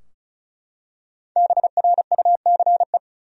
Morse code, heard as a keyed tone, about 36 wpm, 700 Hz. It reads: BRUCE
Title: USA Common Names - 35wpm